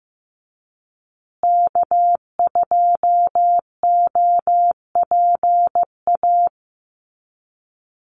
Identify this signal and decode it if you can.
Morse code, heard as a keyed tone, about 15 wpm, 700 Hz. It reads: K2OPA
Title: USA Call Signs Repeated 3x 15wpm